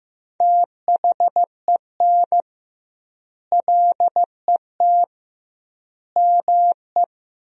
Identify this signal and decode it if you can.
Morse code, heard as a keyed tone, about 15 wpm, 700 Hz. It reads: THEN LET ME
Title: 2x Word-Spacing - Sentences from Top 300 Words - 15wpm